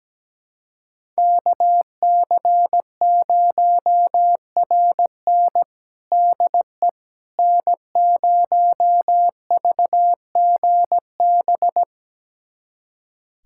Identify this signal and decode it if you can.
Morse code, heard as a keyed tone, about 17 wpm, 700 Hz. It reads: KC0RN DE N0VGB
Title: USA Callsign DE Callsign 17wpm